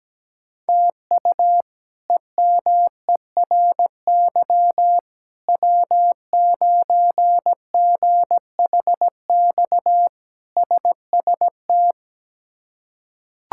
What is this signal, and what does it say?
Morse code, heard as a keyed tone, about 17 wpm, 700 Hz. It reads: TU EMERY W9GHX SST
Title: K1USN SST QSO Practice - 17wpm